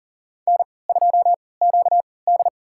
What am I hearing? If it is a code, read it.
Morse code, heard as a keyed tone, about 40 wpm, spaced as for 23 wpm, 700 Hz. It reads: N2QD
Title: USA Call Signs - 4 Characters - 3x Character Spacing - 40wpm